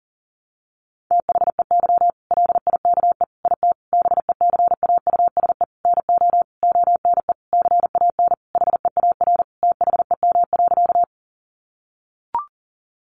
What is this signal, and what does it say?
Morse code, heard as a keyed tone, about 40 wpm, 700 Hz. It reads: THEY LIKE IT BECAUSE NO ONE CAN HEAR THEM.
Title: Sentences from Top 600 Words 40wpm